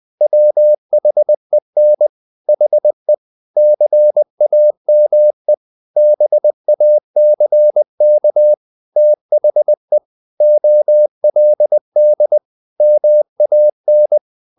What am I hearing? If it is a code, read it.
Morse code, heard as a keyed tone, about 20 wpm, 600 Hz. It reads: WHEN HE CAME BACK THE OLD MAN
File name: the old man and the sea / the old man and the sea part 1 20wpm